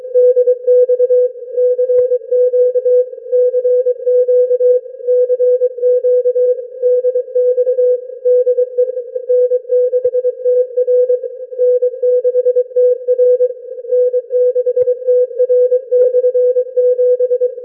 Morse code, heard as a keyed tone, about 23 wpm, 500 Hz. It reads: DX CQ CQ CQ DX DE N6TR N6TR N6TR/7
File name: n6tr / n6tr 7 25dec2007atOZ1LXJ